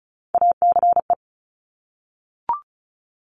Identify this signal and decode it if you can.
Morse code, heard as a keyed tone, about 36 wpm, 700 Hz. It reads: ACE
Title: Top Three Letter Combinations in Words 35wpm